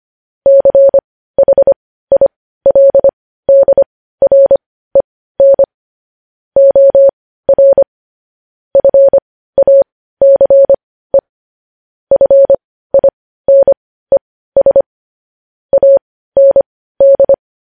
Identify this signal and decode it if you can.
Morse code, heard as a keyed tone, about 25 wpm, spaced as for 15 wpm, 550 Hz. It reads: CHILDREN OR FACE FINES AND
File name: podcast / News headlines-1x-25x15